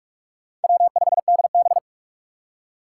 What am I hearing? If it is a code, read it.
Morse code, heard as a keyed tone, about 44 wpm, 700 Hz. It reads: W5DB
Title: USA Call Signs - 6x Character Spacing - 45wpm